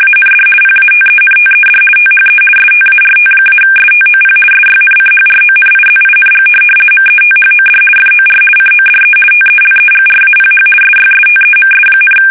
RTTY, about 50 baud. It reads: AND PATENTED THE START-STOP SYNCHRONIZING METHOD FOR CODE TELEGRAPH SYSTEMS,
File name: fsk850-50B / teletype-corp